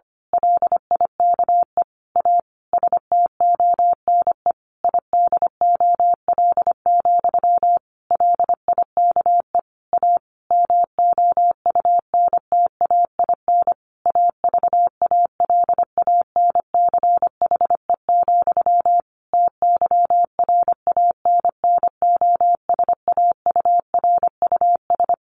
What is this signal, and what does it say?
Morse code, heard as a keyed tone, about 25 wpm, 700 Hz. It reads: LIKE A STONE IDOL, LIKE A MOUNTAIN AVALANCHE, TYRANNOSAURUS
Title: A Sound Of Thunder 25wpm